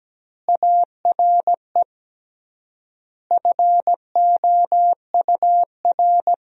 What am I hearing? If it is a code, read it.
Morse code, heard as a keyed tone, about 17 wpm, 700 Hz. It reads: ARE FOUR
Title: 3x Word-Spacing - Sentences from Top 300 Words - 17wpm